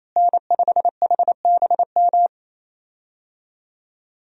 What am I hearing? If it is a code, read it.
Morse code, heard as a keyed tone, about 28 wpm, 700 Hz. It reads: N5HBM
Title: USA Call Signs - 5 Characters - Repeated 3x - 28wpm